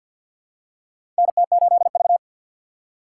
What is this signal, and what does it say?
Morse code, heard as a keyed tone, about 52 wpm, 700 Hz. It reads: NT8V
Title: USA Call Signs - 4 Characters - 2x Character Spacing - 50wpm